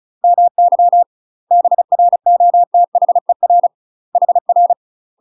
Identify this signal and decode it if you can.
Morse code, heard as a keyed tone, about 35 wpm, 700 Hz. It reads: MY BROTHER HR
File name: UK MASTER QSOs TRAINING / 35WPM / UK MASTER QSOs 0003